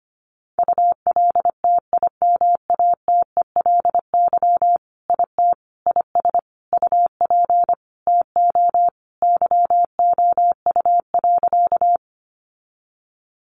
Morse code, heard as a keyed tone, about 25 wpm, 700 Hz. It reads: ULTIMATELY IT IS UP TO YOU.
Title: Sentences from Top 1200 Words 25wpm